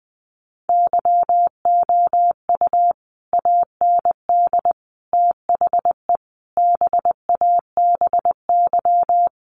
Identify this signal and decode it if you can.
Morse code, heard as a keyed tone, about 20 wpm, 700 Hz. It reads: YOU AND THE BABY
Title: Sentences from Top 1300 Words 20wpm